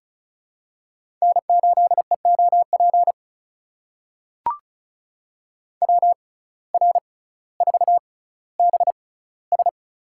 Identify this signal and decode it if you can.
Morse code, heard as a keyed tone, about 36 wpm, spaced as for 18 wpm, 700 Hz. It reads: N8EOP WR4BS
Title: USA Call Signs - 5 Characters - 6x Character Spacing - 35wpm